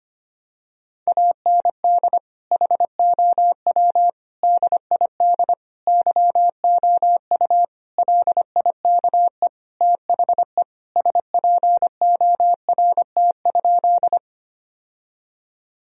Morse code, heard as a keyed tone, about 25 wpm, 700 Hz. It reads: AND HOW DID YOU LIKE THE SPORT?
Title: Sentences from Top 1000 Words - Review Part 4 - 25wpm